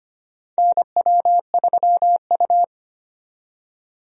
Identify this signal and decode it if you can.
Morse code, heard as a keyed tone, about 25 wpm, 700 Hz. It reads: NW3U
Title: USA Call Signs - 4 Characters - 25wpm